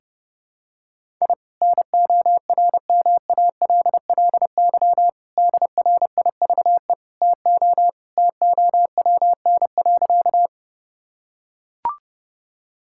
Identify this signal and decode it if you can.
Morse code, heard as a keyed tone, about 30 wpm, 700 Hz. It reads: I NORMALLY DRIVE TO TOWN.
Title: Sentences from Top 1000 Words - Review Part 6 - 30wpm